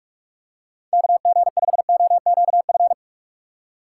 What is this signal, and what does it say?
Morse code, heard as a keyed tone, about 46 wpm, 700 Hz. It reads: KG5OQF
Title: USA Call Signs - 6 Characters - 8x Character Spacing - 45wpm